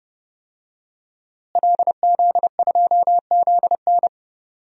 Morse code, heard as a keyed tone, about 30 wpm, 700 Hz. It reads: LZ2ZN
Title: DXCC-100 Call Signs - 5 Characters - Repeated 3x - 30wpm